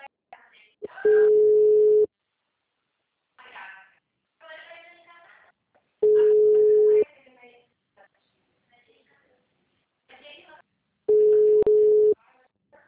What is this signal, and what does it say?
Ringback tone (Europe, 425 Hz), with speech.
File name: out- -2024- - - .35